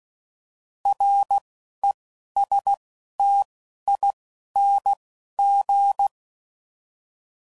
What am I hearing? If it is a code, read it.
Morse code, heard as a keyed tone, about 16 wpm, 800 Hz. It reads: R E S T I N G